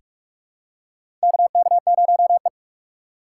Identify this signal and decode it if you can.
Morse code, heard as a keyed tone, about 44 wpm, 700 Hz. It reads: KK0E
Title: USA Call Signs - 4 Characters - 6x Character Spacing - 45wpm